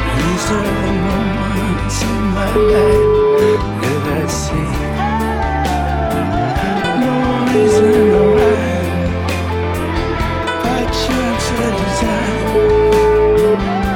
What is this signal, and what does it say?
Ringback tone (Europe, 425 Hz), with noise.